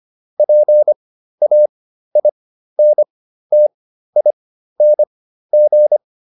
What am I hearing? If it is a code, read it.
Morse code, heard as a keyed tone, about 25 wpm, spaced as for 13 wpm, 600 Hz. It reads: PAINTING
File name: ING8 13